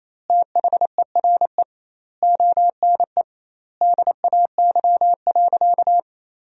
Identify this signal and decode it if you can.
Morse code, heard as a keyed tone, about 28 wpm, 700 Hz. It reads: THERE ONE DAY.
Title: 2x Word-Spacing - Sentences from Top 300 Words - 28wpm